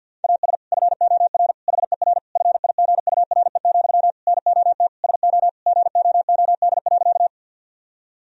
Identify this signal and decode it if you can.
Morse code, heard as a keyed tone, about 50 wpm, 700 Hz. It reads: AS FOR HER FIGURE, NOT SO GOOD.
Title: Sentences from Top 1000 Words - Review Part 2 - 50wpm